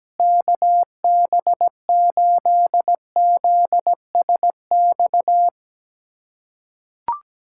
Morse code, heard as a keyed tone, about 17 wpm, 700 Hz. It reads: KB8ZSX
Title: USA Callsign DE Callsign - 6 Characters - 17wpm